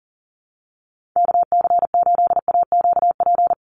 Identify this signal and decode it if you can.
Morse code, heard as a keyed tone, about 40 wpm, 700 Hz. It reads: KC8AQP
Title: USA Call Signs 40wpm